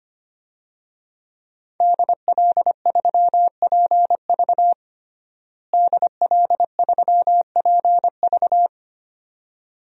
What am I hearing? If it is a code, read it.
Morse code, heard as a keyed tone, about 25 wpm, 700 Hz. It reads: DL3PV DL3PV
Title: DXCC-100 Call Signs - 5 Characters - Repeated 2x - 25wpm